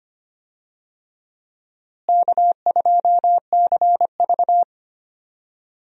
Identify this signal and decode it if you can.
Morse code, heard as a keyed tone, about 25 wpm, 700 Hz. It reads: K2CV
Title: USA Call Signs - 4 Characters - Repeated 3x - 25wpm